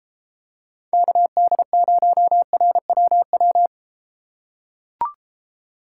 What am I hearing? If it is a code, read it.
Morse code, heard as a keyed tone, about 32 wpm, 700 Hz. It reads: KD0RWW